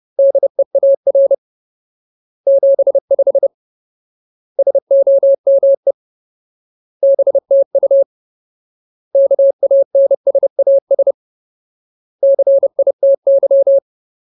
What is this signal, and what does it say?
Morse code, heard as a keyed tone, about 30 wpm, 550 Hz. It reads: DEAR 75 SOME BTU KANSAS CITY
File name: Ultimate Word List / 30WPM Ultimate